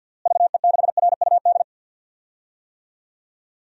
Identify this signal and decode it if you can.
Morse code, heard as a keyed tone, about 52 wpm, 700 Hz. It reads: VE6RUD
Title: DXCC-100 Call Signs - 6 Characters - Repeated 3x - 50wpm